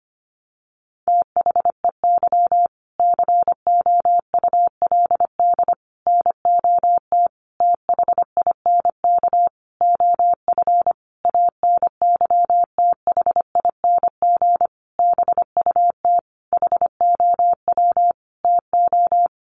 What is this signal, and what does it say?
Morse code, heard as a keyed tone, about 25 wpm, 700 Hz. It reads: THEY COULD NOT THINK OF ANYTHING BUT HOW TO
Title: Sentences from Top 1000 Words 25wpm